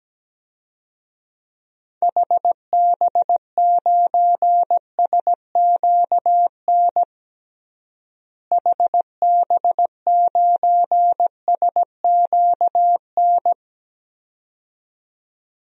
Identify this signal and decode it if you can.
Morse code, heard as a keyed tone, about 17 wpm, 700 Hz. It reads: HB9SQN HB9SQN
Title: DXCC-100 Call Signs - 6 Characters - Repeated 2x - 17wpm